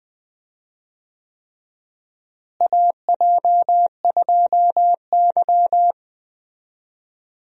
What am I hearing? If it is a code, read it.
Morse code, heard as a keyed tone, about 20 wpm, 700 Hz. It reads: AJ2Y
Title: USA Call Signs - 4 Characters - Repeated 3x - 20wpm